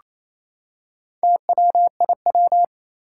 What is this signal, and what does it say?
Morse code, heard as a keyed tone, about 27 wpm, 700 Hz. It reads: TWIW
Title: Four Challenging Letter-Number Combinations 28wpm